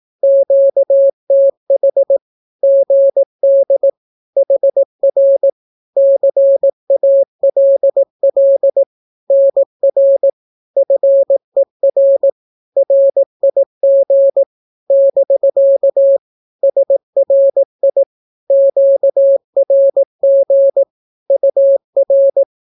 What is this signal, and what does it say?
Morse code, heard as a keyed tone, about 18 wpm, 550 Hz. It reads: QTH GD HR CALL NR FER RIG <BK> SRI QRG UR
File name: naukapodstaw / skroty / skroty